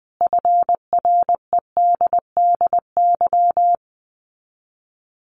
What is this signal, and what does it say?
Morse code, heard as a keyed tone, about 20 wpm, 700 Hz. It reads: FREDDY
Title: Common QSO Phrases 20wpm